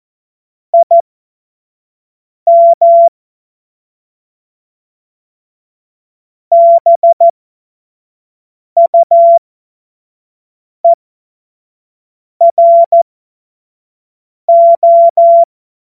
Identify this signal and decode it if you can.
Morse code, heard as a keyed tone, about 14 wpm, spaced as for 5 wpm, 680 Hz. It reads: IM BUERO
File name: Witz-07-Oktober-2025-25-F70